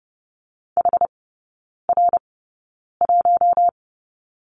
Morse code, heard as a keyed tone, about 30 wpm, spaced as for 9 wpm, 700 Hz. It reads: HR1
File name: Lesson 19e - Letter w - Three Characters 30wpm